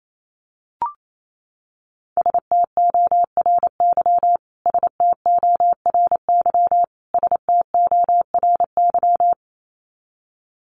Morse code, heard as a keyed tone, about 28 wpm, 700 Hz. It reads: STORY STORY STORY
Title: Top 500 Words Repeated 28wpm